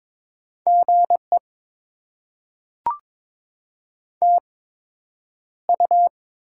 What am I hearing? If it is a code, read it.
Morse code, heard as a keyed tone, about 22 wpm, spaced as for 9 wpm, 700 Hz. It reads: GE TU
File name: Top 20 QSO Elements - 8x Character Spacing - 22wpm